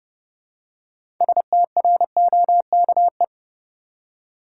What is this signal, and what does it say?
Morse code, heard as a keyed tone, about 30 wpm, 700 Hz. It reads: STROKE
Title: Six Letter Words - 30wpm